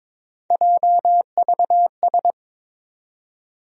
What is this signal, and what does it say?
Morse code, heard as a keyed tone, about 21 wpm, 700 Hz. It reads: JVS
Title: USA Call Sign Postfixes - 3 Letters 22wpm